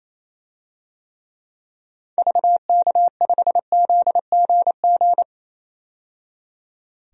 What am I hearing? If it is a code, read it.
Morse code, heard as a keyed tone, about 28 wpm, 700 Hz. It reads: VK5ZGG
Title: DXCC-100 Call Signs - 6 Characters - 28wpm